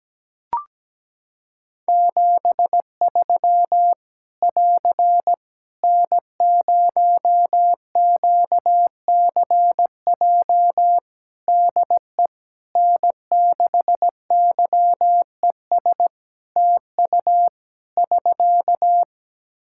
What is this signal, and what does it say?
Morse code, heard as a keyed tone, about 17 wpm, 700 Hz. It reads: 73 <AR> N0QCJ DE N6YES TU <SK>
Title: K1USN SST QSO Practice 17wpm